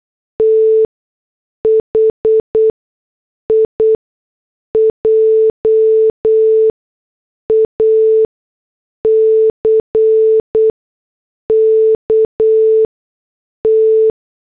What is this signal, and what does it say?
Morse code, heard as a keyed tone, about 8 wpm, spaced as for 6 wpm, 440 Hz. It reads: T HIJACK T